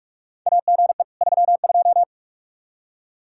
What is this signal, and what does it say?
Morse code, heard as a keyed tone, about 46 wpm, 700 Hz. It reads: AGE 32
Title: K1USN SST QSO Practice 45wpm